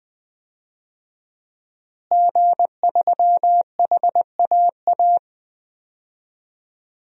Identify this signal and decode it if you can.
Morse code, heard as a keyed tone, about 21 wpm, 700 Hz. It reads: G3HAA
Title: DXCC-100 Call Signs - 5 Characters - Repeated 3x - 20wpm